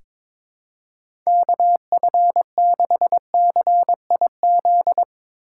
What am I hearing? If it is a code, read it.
Morse code, heard as a keyed tone, about 22 wpm, 700 Hz. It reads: KF6CIZ
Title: USA Call Signs - 6 Characters - 3x Character Spacing - 22wpm